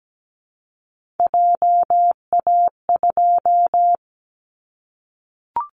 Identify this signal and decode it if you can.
Morse code, heard as a keyed tone, about 17 wpm, 700 Hz. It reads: JA2
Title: Callsign Prefixes for Top 50 DXCC - 17wpm